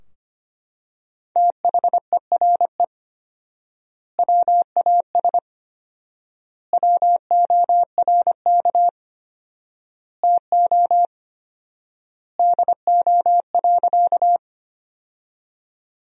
Morse code, heard as a keyed tone, about 25 wpm, 700 Hz. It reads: THERE WAS WORK TO DO.
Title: 4x Word-Spacing - Sentences from Top 200 Words - 25wpm